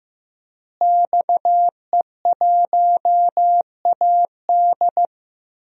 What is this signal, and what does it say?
Morse code, heard as a keyed tone, about 15 wpm, 700 Hz. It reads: XE1AD
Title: DXCC-100 Call Signs - 5 Characters - Repeated 2x - 15wpm